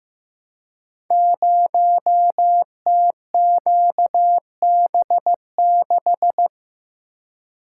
Morse code, heard as a keyed tone, about 15 wpm, 700 Hz. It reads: 0TQB6
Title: Code Groups - 5 Letter:Numbers - 15wpm